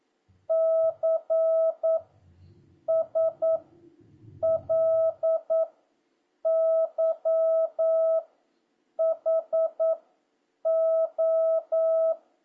Morse code, heard as a keyed tone, about 9 wpm, 650 Hz. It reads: C S L Y H O